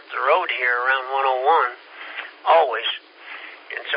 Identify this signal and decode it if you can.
Emergency Alert System (SAME) tones, heard as harsh buzzing bursts, with speech. It is heard from 0.5 to 3.6 s.
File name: SimplexLogic